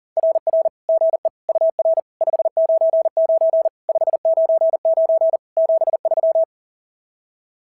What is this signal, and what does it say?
Morse code, heard as a keyed tone, about 41 wpm, 620 Hz. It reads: RR GE UR 599 599 73
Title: SOTA QSO Practice 40wpm